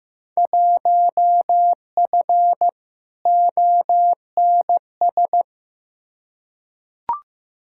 Morse code, heard as a keyed tone, about 15 wpm, 700 Hz. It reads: 1F ONS
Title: Field Day Exchanges Encore - 15wpm